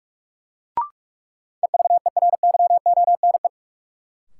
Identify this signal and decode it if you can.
Morse code, heard as a keyed tone, about 43 wpm, 700 Hz. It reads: EVERYONE
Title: Sentences from Top 800 Words - New Word Drill - 45wpm